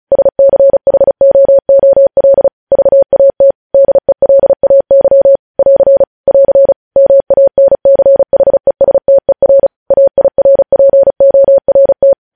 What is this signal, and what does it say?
Morse code, heard as a keyed tone, about 36 wpm, 550 Hz. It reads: SCHOOL VAT DELAY <AR> <AR> MANCHESTER AIRPORT